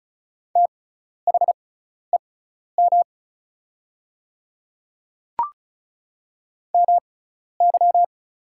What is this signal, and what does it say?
Morse code, heard as a keyed tone, about 36 wpm, spaced as for 12 wpm, 700 Hz. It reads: THEM MY
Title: 6x Character Spacing - Top 100 Words - 35wpm